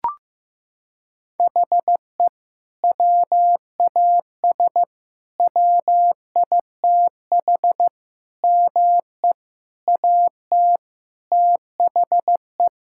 Morse code, heard as a keyed tone, about 14 wpm, 700 Hz. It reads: HE WAS WITH ME AT THE
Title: Sentences from Top 1000 Words - Review Part 5 - 15wpm